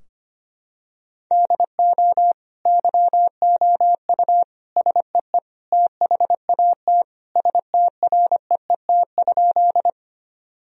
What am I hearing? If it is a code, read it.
Morse code, heard as a keyed tone, about 25 wpm, 700 Hz. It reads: DO YOU SEE THAT STREET?
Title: Sentences from Top 1000 Words - Review Part 3 - 25wpm